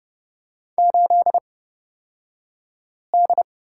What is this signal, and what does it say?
Morse code, heard as a keyed tone, about 30 wpm, spaced as for 5 wpm, 700 Hz. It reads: 8D
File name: Lesson 38d - Letter q - Two Characters 30wpm